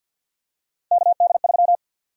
Morse code, heard as a keyed tone, about 52 wpm, 700 Hz. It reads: KD3
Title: USA Call Sign Prefixes - Repeated 2x - 50wpm